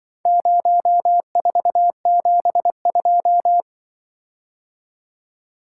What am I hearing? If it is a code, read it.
Morse code, heard as a keyed tone, about 23 wpm, 700 Hz. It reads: 0472
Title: Speed-Racing - Four Letter-Number Combinations - 20wpm